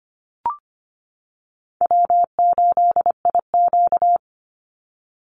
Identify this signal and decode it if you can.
Morse code, heard as a keyed tone, about 25 wpm, 700 Hz. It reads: W8IQ
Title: Easy Call Signs - 25wpm